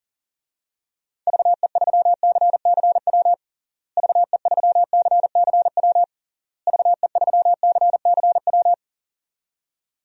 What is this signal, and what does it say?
Morse code, heard as a keyed tone, about 40 wpm, 700 Hz. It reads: VE3CCW VE3CCW VE3CCW
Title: DXCC-100 Call Signs - 6 Characters - Repeated 3x - 40wpm